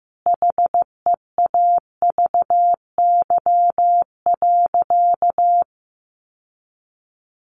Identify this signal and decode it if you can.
Morse code, heard as a keyed tone, about 15 wpm, 700 Hz. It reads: HEAVY.
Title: Sentences from Top 1100 Words 15wpm